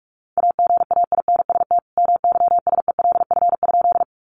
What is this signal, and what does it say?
Morse code, heard as a keyed tone, about 44 wpm, 700 Hz. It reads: AGAINST MYSELF?
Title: Sentences from Top 1300 Words 45wpm